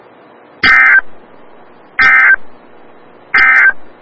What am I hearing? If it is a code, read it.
Emergency Alert System (SAME) tones, heard as harsh buzzing bursts. It is heard from 0.5 to 3.8 s.
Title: Des Moines, IA NOAA Weather Radio - WXL57